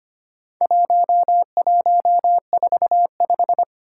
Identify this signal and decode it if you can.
Morse code, heard as a keyed tone, about 25 wpm, 700 Hz. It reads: 1145
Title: Four Challenging Letter-Number Combinations 25wpm